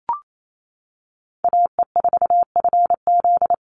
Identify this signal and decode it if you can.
Morse code, heard as a keyed tone, about 28 wpm, 700 Hz. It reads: AE4FZ
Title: NA SOTA Chaser Callsigns 28wpm